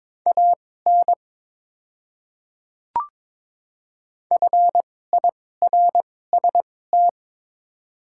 Morse code, heard as a keyed tone, about 22 wpm, spaced as for 16 wpm, 700 Hz. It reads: AN FIRST
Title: 2x Character Spacing - Top 200 Words - 22wpm